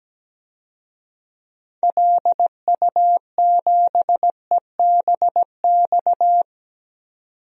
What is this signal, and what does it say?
Morse code, heard as a keyed tone, about 17 wpm, 700 Hz. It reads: LU7EBX
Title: DXCC-100 Call Signs - 6 Characters - 17wpm